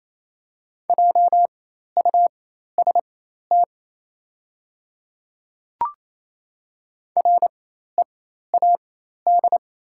Morse code, heard as a keyed tone, about 28 wpm, spaced as for 13 wpm, 700 Hz. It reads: JUST READ
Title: 4x Character Spacing - Top 300 Words - 28wpm